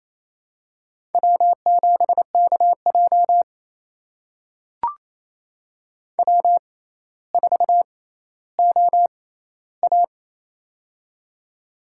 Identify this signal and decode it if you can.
Morse code, heard as a keyed tone, about 28 wpm, spaced as for 14 wpm, 700 Hz. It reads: W7KJ W4OA
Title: USA Call Signs - 4 Characters - 6x Character Spacing - 28wpm